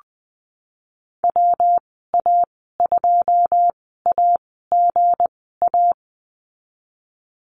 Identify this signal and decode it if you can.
Morse code, heard as a keyed tone, about 19 wpm, spaced as for 14 wpm, 700 Hz. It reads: WA2AGA
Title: 231 Lesson 37g - Letter x - Call Signs 20wpm